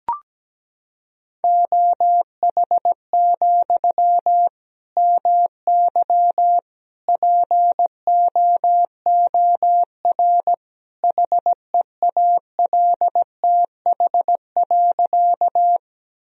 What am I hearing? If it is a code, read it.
Morse code, heard as a keyed tone, about 17 wpm, 700 Hz. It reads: OH, MY POOR HEALTH.
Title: Sentences from Top 1000 Words - Review Part 3 - 17wpm